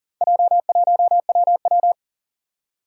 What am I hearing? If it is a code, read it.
Morse code, heard as a keyed tone, about 40 wpm, 700 Hz. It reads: J1WW
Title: Four Challenging Letter-Number Combinations 40wpm